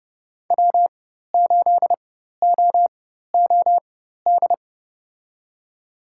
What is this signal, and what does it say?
Morse code, heard as a keyed tone, about 30 wpm, spaced as for 14 wpm, 700 Hz. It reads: W8OOD
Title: USA Call Signs - 5 Characters - 4x Character Spacing - 30wpm